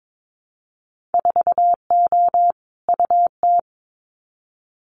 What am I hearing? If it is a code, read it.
Morse code, heard as a keyed tone, about 22 wpm, 700 Hz. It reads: 4O UT
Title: Winter Field Day Exchanges Encore 22wpm